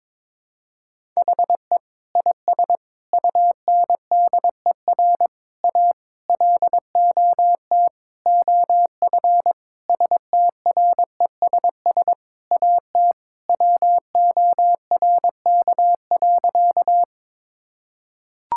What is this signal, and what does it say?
Morse code, heard as a keyed tone, about 22 wpm, 700 Hz. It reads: HE IS UNDER A LOT OF STRESS AT WORK.
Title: Sentences from Top 1000 Words - Review Part 6 - 22wpm